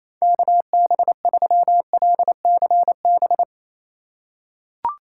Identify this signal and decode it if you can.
Morse code, heard as a keyed tone, about 28 wpm, 700 Hz. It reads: KB3LCB
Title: USA Call Signs - 6 Characters - 4x Character Spacing - 28wpm